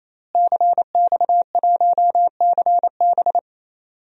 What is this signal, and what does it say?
Morse code, heard as a keyed tone, about 28 wpm, 700 Hz. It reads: CX1CB
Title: DXCC-100 Call Signs - 5 Characters - Repeated 3x - 28wpm